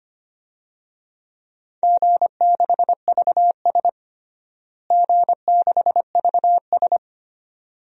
Morse code, heard as a keyed tone, about 25 wpm, 700 Hz. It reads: G6VS G6VS
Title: DXCC-100 Call Signs - 4 Characters - Repeated 2x - 25wpm